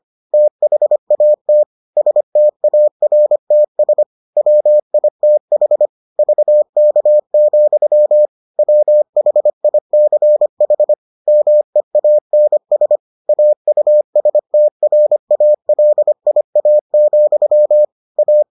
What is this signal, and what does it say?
Morse code, heard as a keyed tone, about 25 wpm, 600 Hz. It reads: THAT STARTS WITH VK, WHICH MEANS AUSTRALIA, A